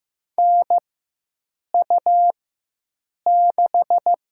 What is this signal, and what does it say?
Morse code, heard as a keyed tone, about 15 wpm, 700 Hz. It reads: N U 6
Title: USA Call Signs - 4 Characters - 4x Character Spacing - 15wpm